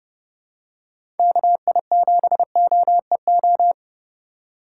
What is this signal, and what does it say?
Morse code, heard as a keyed tone, about 31 wpm, 700 Hz. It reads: KI7OEO
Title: USA Call Signs - 6 Characters - 4x Character Spacing - 30wpm